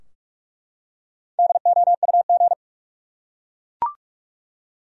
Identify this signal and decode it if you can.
Morse code, heard as a keyed tone, about 44 wpm, 700 Hz. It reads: DOUG
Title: USA Common Names - Max 5 Letters - 45wpm